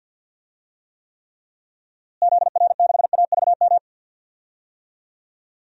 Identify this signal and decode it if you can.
Morse code, heard as a keyed tone, about 47 wpm, 700 Hz. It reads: ZR6AVM
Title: DXCC-100 Call Signs - 6 Characters - Repeated 3x - 50wpm